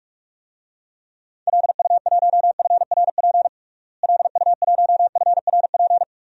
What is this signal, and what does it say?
Morse code, heard as a keyed tone, about 46 wpm, 700 Hz. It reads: LU1FRP LU1FRP
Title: DXCC-100 Call Signs - 6 Characters - Repeated 2x - 45wpm